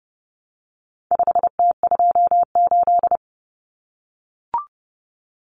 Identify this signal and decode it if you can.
Morse code, heard as a keyed tone, about 30 wpm, 700 Hz. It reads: 5T28
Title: Common Cut Numbers - 4-Digit 30wpm